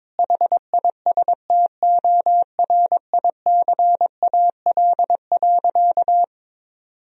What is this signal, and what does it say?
Morse code, heard as a keyed tone, about 22 wpm, 700 Hz. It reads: HISTORICAL.
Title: Sentences from Top 1000 Words - Review Part 4 - 22wpm